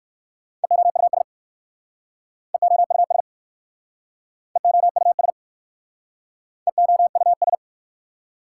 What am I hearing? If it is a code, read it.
Morse code, heard as a keyed tone, about 62 wpm, 690 Hz. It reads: EOUS EOUS EOUS EOUS